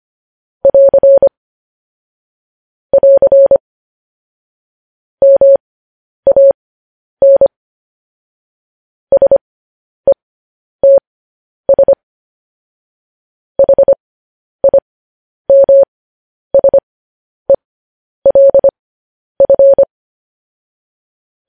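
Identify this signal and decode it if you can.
Morse code, heard as a keyed tone, about 25 wpm, spaced as for 10 wpm, 550 Hz. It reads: <AR> <AR> MAN SETS HIMSELF
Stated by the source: Morse code news headlines practice, at 10 WPM.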